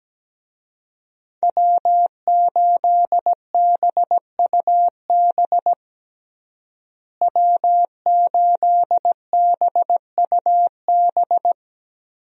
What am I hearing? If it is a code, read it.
Morse code, heard as a keyed tone, about 17 wpm, 700 Hz. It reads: W8BUB W8BUB
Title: USA Call Signs - 5 Characters - Repeated 2x - 17wpm